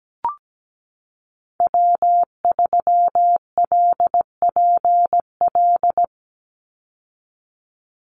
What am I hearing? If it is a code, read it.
Morse code, heard as a keyed tone, about 17 wpm, 700 Hz. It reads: W3LPL
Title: Call Signs Repeated 3x - 17wpm